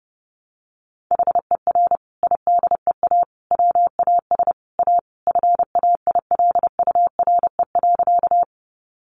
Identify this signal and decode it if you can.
Morse code, heard as a keyed tone, about 30 wpm, 700 Hz. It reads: HER IDEA WAS A FAILURE.
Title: Sentences from Top 900 Words 30wpm